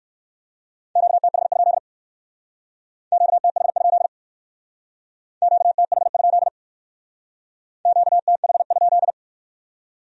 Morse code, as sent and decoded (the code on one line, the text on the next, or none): --.- - .... ..--.. / --.- - .... ..--.. / --.- - .... ..--.. / --.- - .... ..--..
QTH? QTH? QTH? QTH?